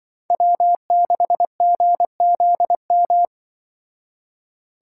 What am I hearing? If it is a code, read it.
Morse code, heard as a keyed tone, about 24 wpm, 700 Hz. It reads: W6GZM
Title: Speed Racing - USA 1x3 Call Signs 20wpm